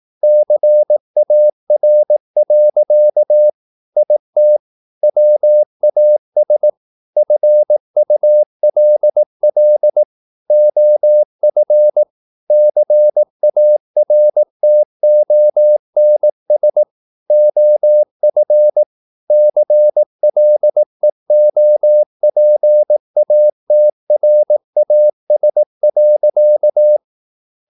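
Morse code, heard as a keyed tone, about 18 wpm, 600 Hz. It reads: CAR. IT WAS FULL OF CARTONS OF CLEOPATRAS.
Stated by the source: Morse practice at 18 wpm, English text, from CW med Gnister.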